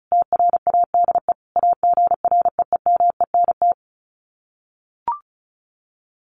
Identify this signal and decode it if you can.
Morse code, heard as a keyed tone, about 34 wpm, 700 Hz. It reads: TRADE AGREEMENT
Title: Sets of 5 Words Encore 35wpm